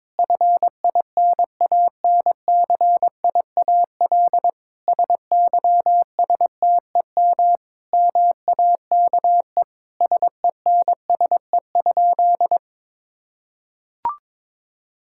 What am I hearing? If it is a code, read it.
Morse code, heard as a keyed tone, about 22 wpm, 700 Hz. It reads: FINANCIAL SYSTEM MAKE SENSE?
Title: Sentences from Top 1000 Words - Review Part 5 - 22wpm